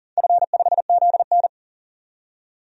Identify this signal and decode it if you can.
Morse code, heard as a keyed tone, about 40 wpm, 700 Hz. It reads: F5ZN
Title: DXCC-100 Call Signs - 4 Characters - 40wpm